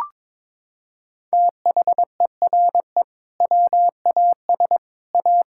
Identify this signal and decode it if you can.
Morse code, heard as a keyed tone, about 21 wpm, 700 Hz. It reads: THERE WAS A
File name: practice / 22wpm / Sentences with 3-Word Phrases - 22wpm